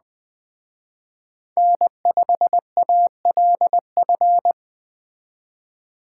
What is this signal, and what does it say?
Morse code, heard as a keyed tone, about 21 wpm, 700 Hz. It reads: N5ALF